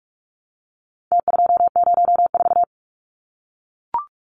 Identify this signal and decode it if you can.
Morse code, heard as a keyed tone, about 44 wpm, 700 Hz. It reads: T204
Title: Numbers - Contest Style - 4-Digit 45wpm